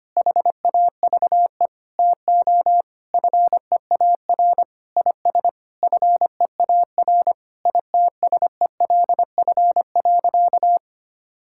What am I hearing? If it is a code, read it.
Morse code, heard as a keyed tone, about 25 wpm, 700 Hz. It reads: HAVE TO FEAR IS FEAR ITSELF.
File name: practice / 25wpm / Sentences with 9-Word Phrases - 25wpm